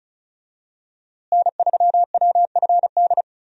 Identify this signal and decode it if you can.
Morse code, heard as a keyed tone, about 34 wpm, 700 Hz. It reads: N3WFD
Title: USA Call Signs - 5 Characters - Repeated 2x - 35wpm